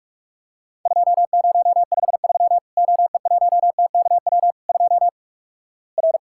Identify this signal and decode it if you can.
Morse code, heard as a keyed tone, about 46 wpm, 700 Hz. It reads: 2053 OE1TKW 2 R
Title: Worked All Europe DX Contest - QTC Exchange - 45wpm